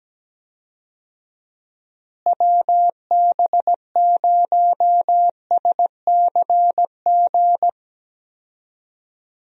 Morse code, heard as a keyed tone, about 17 wpm, 700 Hz. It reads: WB0SCG
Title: USA Call Signs - 6 Characters - Repeated 2x - 17wpm